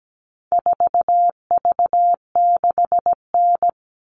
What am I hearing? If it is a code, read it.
Morse code, heard as a keyed tone, about 17 wpm, 700 Hz. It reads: 4V6N
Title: All Cut Numbers - 4-Digit 17wpm